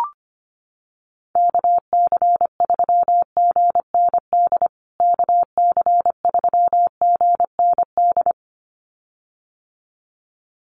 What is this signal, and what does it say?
Morse code, heard as a keyed tone, about 25 wpm, 700 Hz. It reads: KC3GND KC3GND
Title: USA Call Signs Repeated 2x 25wpm